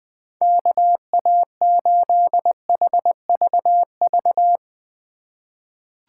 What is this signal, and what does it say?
Morse code, heard as a keyed tone, about 21 wpm, 700 Hz. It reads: KA8HVV
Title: USA Call Signs - 4x Character Spacing - 20wpm